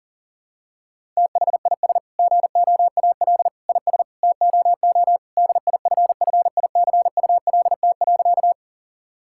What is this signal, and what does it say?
Morse code, heard as a keyed tone, about 37 wpm, 700 Hz. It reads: THIS GOAL IS TOO DIFFICULT.
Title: Sentences from Top 1000 Words - Review Part 5 - 40wpm